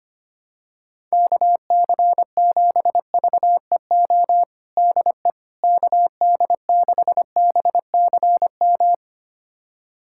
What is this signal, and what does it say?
Morse code, heard as a keyed tone, about 25 wpm, 700 Hz. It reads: KC7VEO DE KD6BCM
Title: USA Callsign DE Callsign - 6 Characters - 25wpm